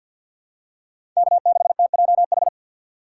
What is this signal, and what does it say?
Morse code, heard as a keyed tone, about 53 wpm, 700 Hz. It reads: K6TJH
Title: USA Call Signs - 5 Characters - 4x Character Spacing - 50wpm